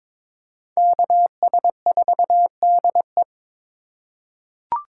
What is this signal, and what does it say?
Morse code, heard as a keyed tone, about 22 wpm, 700 Hz. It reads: KS4DE
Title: USA Call Signs - 5 Characters - 3x Character Spacing - 22wpm